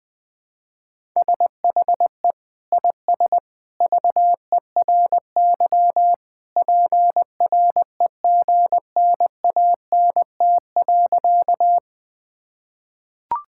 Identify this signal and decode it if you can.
Morse code, heard as a keyed tone, about 20 wpm, 700 Hz. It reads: SHE IS VERY PREGNANT.
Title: Sentences from Top 1000 Words - Review Part 6 - 20wpm